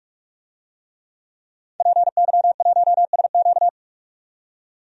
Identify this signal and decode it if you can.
Morse code, heard as a keyed tone, about 46 wpm, 700 Hz. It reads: PY1SQ
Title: DXCC-100 Call Signs - 5 Characters - 45wpm